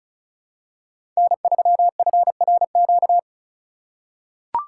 Morse code, heard as a keyed tone, about 34 wpm, 700 Hz. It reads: N3FRQ
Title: USA Call Signs - 4x Character Spacing - 35wpm